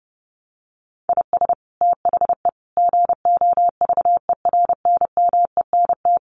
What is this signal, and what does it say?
Morse code, heard as a keyed tone, about 30 wpm, 700 Hz. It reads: IS THE GOVERNMENT
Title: Sentences from Top 1500 Words 30wpm